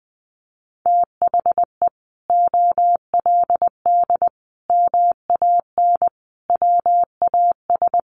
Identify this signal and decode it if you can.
Morse code, heard as a keyed tone, about 20 wpm, 700 Hz. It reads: THE OLD MAN WAS
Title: Sentences from Top 1100 Words 20wpm